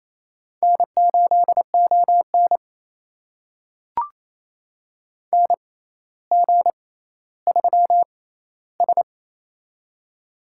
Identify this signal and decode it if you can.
Morse code, heard as a keyed tone, about 28 wpm, spaced as for 14 wpm, 700 Hz. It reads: N8ON NG3S
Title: USA Call Signs - 6x Character Spacing - 28wpm